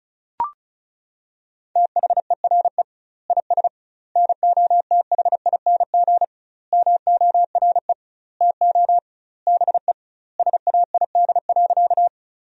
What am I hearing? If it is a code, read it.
Morse code, heard as a keyed tone, about 34 wpm, 700 Hz. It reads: THERE IS NOTHING MORE TO BE SAID.
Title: 2x Word-Spacing - Sentences from Top 500 Words - 35wpm